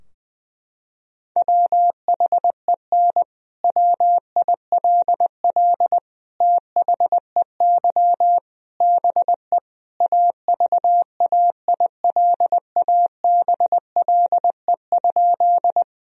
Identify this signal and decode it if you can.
Morse code, heard as a keyed tone, about 21 wpm, 700 Hz. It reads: WHEN WILL THEY BE AVAILABLE?
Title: Sentences from Top 1000 Words - Review Part 3 - 20wpm